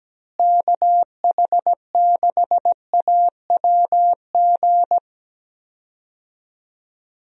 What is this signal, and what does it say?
Morse code, heard as a keyed tone, about 17 wpm, 700 Hz. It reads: KH6AWG
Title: DXCC-100 Call Signs - 6 Characters - Repeated 2x - 17wpm